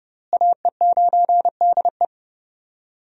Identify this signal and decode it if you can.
Morse code, heard as a keyed tone, about 29 wpm, 700 Hz. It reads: AE9DE
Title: USA Call Signs - 5 Characters - Repeated 2x - 30wpm